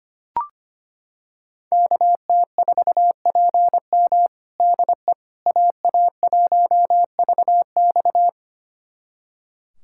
Morse code, heard as a keyed tone, about 25 wpm, 700 Hz. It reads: KT4PM DE AA1VX
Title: USA Callsign DE Callsign - 5 Characters - 25wpm